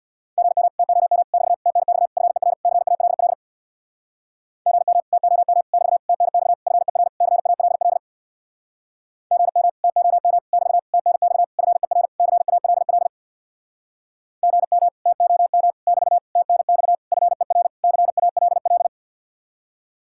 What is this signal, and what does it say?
Morse code, heard as a keyed tone, about 67 wpm, 700 Hz. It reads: GM TOM <BT> TNX FER CALL GM TOM <BT> TNX FER CALL GM TOM <BT> TNX FER CALL GM TOM <BT> TNX FER CALL